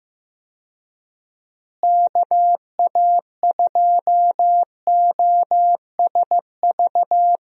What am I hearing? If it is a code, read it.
Morse code, heard as a keyed tone, about 15 wpm, 700 Hz. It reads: KA2OSV
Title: USA Call Signs - 6 Characters - Repeated 2x - 15wpm